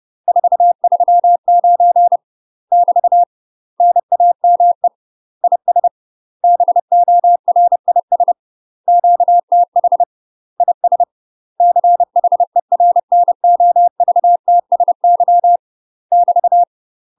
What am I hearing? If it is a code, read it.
Morse code, heard as a keyed tone, about 30 wpm, 700 Hz. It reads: 439 <BT> NAME IS BORIS QTH IS CHERNOVTSY <BT>
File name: UK MASTER QSOs TRAINING / 30WPM / UK MASTER QSOs 0048